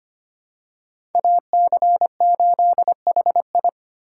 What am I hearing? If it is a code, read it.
Morse code, heard as a keyed tone, about 25 wpm, 700 Hz. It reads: AC8HI
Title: USA Call Signs - 5 Characters - 3x Character Spacing - 25wpm